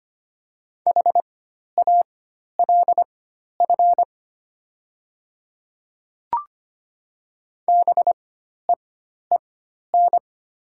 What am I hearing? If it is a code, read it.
Morse code, heard as a keyed tone, about 25 wpm, spaced as for 12 wpm, 700 Hz. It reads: HALF BEEN
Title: 4x Character Spacing - Top 500 Words - 25wpm